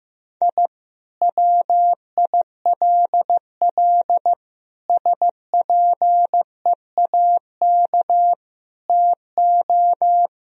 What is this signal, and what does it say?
Morse code, heard as a keyed tone, about 15 wpm, 700 Hz. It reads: I WILL SPEAK TO
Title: Sentences from Top 1000 Words - Review Part 2 - 15wpm